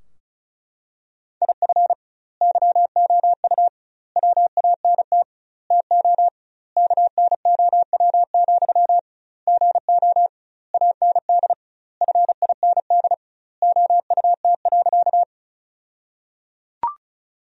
Morse code, heard as a keyed tone, about 36 wpm, 700 Hz. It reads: IF YOU WANT TO KNOW, GO AND FIND OUT.
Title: 2x Word-Spacing - Sentences from Top 200 Words - 35wpm